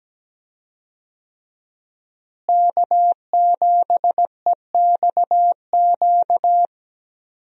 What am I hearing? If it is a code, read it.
Morse code, heard as a keyed tone, about 17 wpm, 700 Hz. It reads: K7EXQ
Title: USA Call Signs - 5 Characters - Repeated 3x - 17wpm